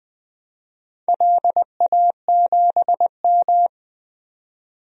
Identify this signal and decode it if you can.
Morse code, heard as a keyed tone, about 20 wpm, 700 Hz. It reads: LA7M
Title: DXCC-100 Call Signs - 4 Characters - 20wpm